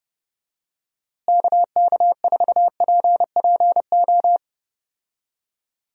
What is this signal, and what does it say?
Morse code, heard as a keyed tone, about 30 wpm, 700 Hz. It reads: KK4PPO
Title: USA Call Signs - 6 Characters - Repeated 2x - 30wpm